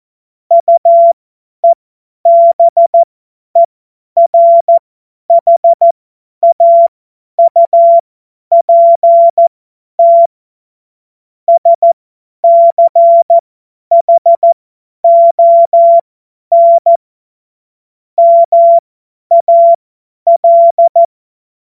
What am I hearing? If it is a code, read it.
Morse code, heard as a keyed tone, about 14 wpm, spaced as for 10 wpm, 680 Hz. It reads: UEBERHAUPT SCHON MAL